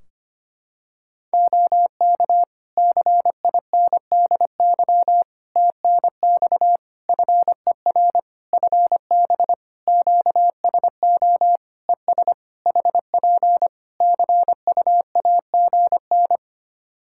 Morse code, heard as a keyed tone, about 25 wpm, 700 Hz. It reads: OK CINDY TNX FER FB QSO ES HP CUAGN
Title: K1USN SST QSO Practice 25wpm